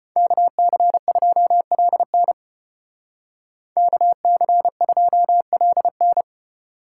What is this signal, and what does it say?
Morse code, heard as a keyed tone, about 32 wpm, 700 Hz. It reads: KC2LN KC2LN